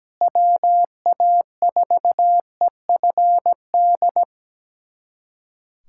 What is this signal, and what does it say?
Morse code, heard as a keyed tone, about 17 wpm, 700 Hz. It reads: WA4EFD
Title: USA Callsign DE Callsign 17wpm